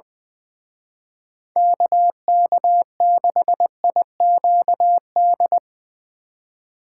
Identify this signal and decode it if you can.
Morse code, heard as a keyed tone, about 22 wpm, 700 Hz. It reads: KK6IQD
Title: Speed Racing - USA 2x3 Call Signs 20wpm